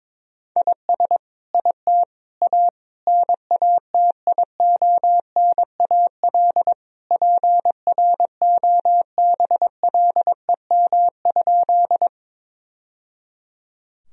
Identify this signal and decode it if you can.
Morse code, heard as a keyed tone, about 22 wpm, 700 Hz. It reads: IS IT A NATIONAL PROBLEM?
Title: Sentences from Top 1000 Words - Review Part 4 - 22wpm